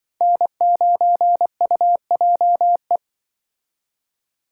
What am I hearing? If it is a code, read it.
Morse code, heard as a keyed tone, about 24 wpm, 700 Hz. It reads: N9UJE
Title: Speed Racing - USA 1x3 Call Signs 20wpm